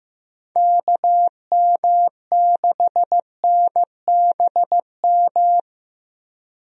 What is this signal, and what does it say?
Morse code, heard as a keyed tone, about 15 wpm, 700 Hz. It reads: KM6NBM
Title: USA Call Signs - 6 Characters - Repeated 2x - 15wpm